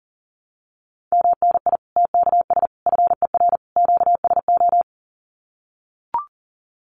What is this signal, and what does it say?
Morse code, heard as a keyed tone, about 40 wpm, 700 Hz. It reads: MNI TKS FER QSO
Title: Common QSO Phrases 40wpm